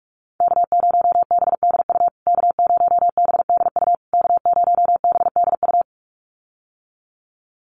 Morse code, heard as a keyed tone, about 44 wpm, 700 Hz. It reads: K0BDU K0BDU K0BDU
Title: USA Call Signs Repeated 3x 45wpm